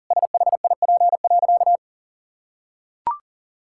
Morse code, heard as a keyed tone, about 40 wpm, 700 Hz. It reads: SHIP.
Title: Sentences from Top 1000 Words - Review Part 5 - 40wpm